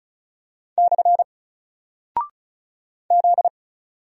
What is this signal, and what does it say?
Morse code, heard as a keyed tone, about 34 wpm, 700 Hz. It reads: / Z
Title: Single Character (Letter:Number:Punctuation:Prosign) Alt 35wpm